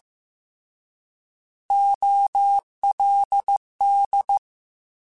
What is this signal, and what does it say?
Morse code, heard as a keyed tone, about 15 wpm, 800 Hz. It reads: OLD